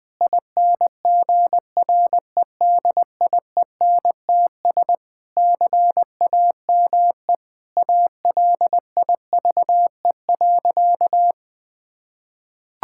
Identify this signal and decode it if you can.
Morse code, heard as a keyed tone, about 20 wpm, 700 Hz. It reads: INGREDIENTS CAME ALIVE.
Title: Sentences with 5-Word Phrases - 20wpm